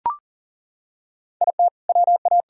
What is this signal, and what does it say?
Morse code, heard as a keyed tone, about 40 wpm, 700 Hz. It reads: IT WA
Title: Sentences from Top 500 Words and Abbreviations 40wpm